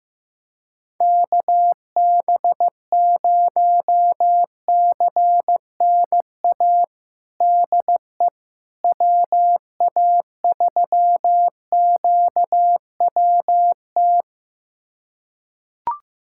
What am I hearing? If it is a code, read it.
Morse code, heard as a keyed tone, about 15 wpm, 700 Hz. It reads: KB0CNA DE WA3QWT
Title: USA Callsign DE Callsign - 6 Characters - 15wpm